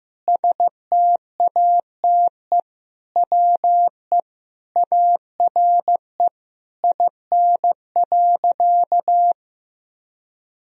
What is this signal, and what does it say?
Morse code, heard as a keyed tone, about 15 wpm, 700 Hz. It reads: STATE WE ARE IN.
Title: Sentences from Top 1000 Words - Review Part 2 - 15wpm